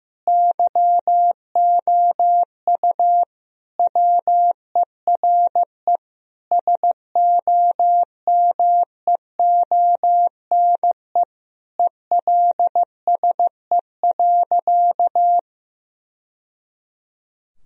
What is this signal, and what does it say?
Morse code, heard as a keyed tone, about 16 wpm, 700 Hz. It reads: YOU WERE SOMEONE ELSE.
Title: Sentences from Top 1000 Words - Review Part 4 - 15wpm